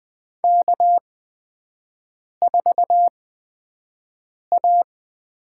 Morse code, heard as a keyed tone, about 20 wpm, spaced as for 5 wpm, 700 Hz. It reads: K4A
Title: USA Call Signs - 5 Characters - 8x Character Spacing - 20wpm